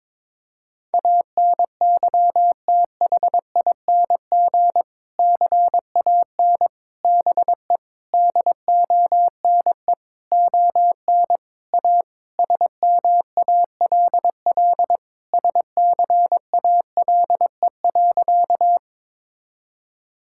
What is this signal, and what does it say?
Morse code, heard as a keyed tone, about 22 wpm, 700 Hz. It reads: ANYTHING CAN BE DONE ON A SMALL SCALE.
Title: Sentences from Top 1000 Words - Review Part 5 - 22wpm